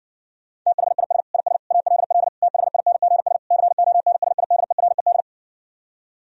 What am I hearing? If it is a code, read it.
Morse code, heard as a keyed tone, about 60 wpm, 700 Hz. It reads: THIS IS ALL THINGS CONSIDERED